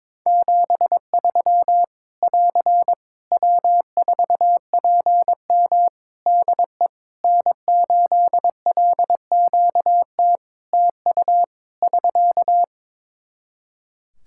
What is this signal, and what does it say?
Morse code, heard as a keyed tone, about 22 wpm, 700 Hz. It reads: 73 <AR> W4PM DE N8LQT TU <SK>
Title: Practice QSO - Exchange 4 - Ending - 22wpm